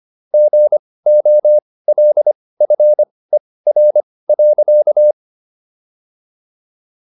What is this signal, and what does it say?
Morse code, heard as a keyed tone, about 25 wpm, spaced as for 18 wpm, 600 Hz. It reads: GOLFER.